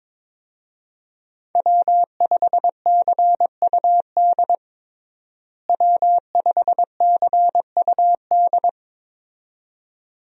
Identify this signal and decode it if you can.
Morse code, heard as a keyed tone, about 21 wpm, 700 Hz. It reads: W5CUD W5CUD
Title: USA Call Signs - 5 Characters - Repeated 2x - 22wpm